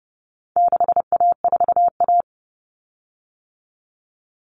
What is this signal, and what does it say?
Morse code, heard as a keyed tone, about 30 wpm, 700 Hz. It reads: 6A4A
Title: Common Cut Numbers - 4-Digit 30wpm